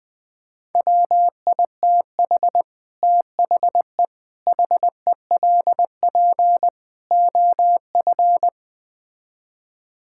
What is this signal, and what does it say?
Morse code, heard as a keyed tone, about 20 wpm, 700 Hz. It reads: WITH THE HELP OF
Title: Build A Phrase - 5-Word Phrases - 20wpm